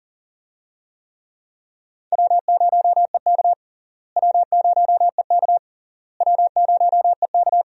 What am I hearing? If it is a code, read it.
Morse code, heard as a keyed tone, about 37 wpm, 700 Hz. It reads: W0EK W0EK W0EK
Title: USA Call Signs - 4 Characters - Repeated 3x - 40wpm